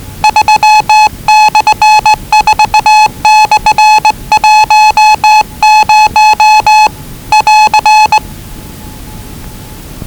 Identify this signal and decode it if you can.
Morse code, heard as a keyed tone, about 20 wpm, 840 Hz. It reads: 3/4/10 <AR>
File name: WellingtonRadioZlwCloseDownOnHfAndMf / 01-Zlw 4277khz 417.5 qrt until 0925z